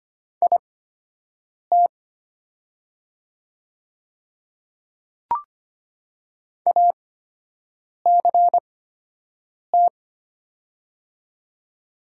Morse code, heard as a keyed tone, about 25 wpm, spaced as for 7 wpm, 700 Hz. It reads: IT ACT
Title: 8x Character Spacing - Top 200 Words - 25wpm